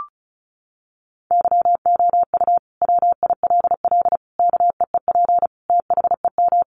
Morse code, heard as a keyed tone, about 34 wpm, 700 Hz. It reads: YOU WILL KEEP THEM
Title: Sentences from Top 1300 Words 35wpm